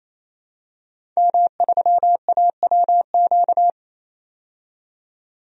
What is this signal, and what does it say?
Morse code, heard as a keyed tone, about 28 wpm, 700 Hz. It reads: M3AWQ
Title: DXCC-100 Call Signs - 5 Characters - 28wpm